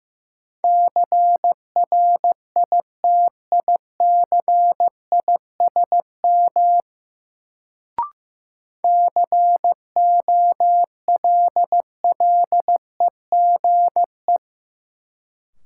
Morse code, heard as a keyed tone, about 15 wpm, 700 Hz. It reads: CRITICISM COLLEGE
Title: Sentences from Top 1300 Words - New Word Drill Alt - 15wpm